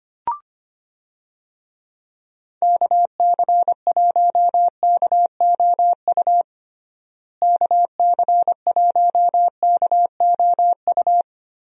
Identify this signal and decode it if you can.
Morse code, heard as a keyed tone, about 25 wpm, 700 Hz. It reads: KC1KOU KC1KOU
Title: USA Call Signs - 6 Characters - Repeated 3x - 25wpm